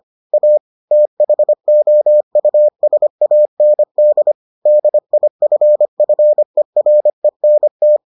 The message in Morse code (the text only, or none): A THOUSAND DIFFERENT